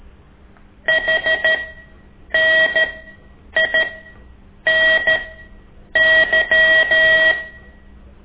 Morse code, heard as a keyed tone, about 13 wpm, spaced as for 8 wpm, 1900 Hz. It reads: HNINY
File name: ik1weg2 / CORSI CW